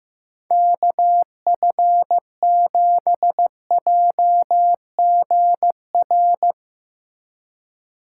Morse code, heard as a keyed tone, about 15 wpm, 700 Hz. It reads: KF7JGR
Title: Speed Racing - USA 2x3 Call Signs 15wpm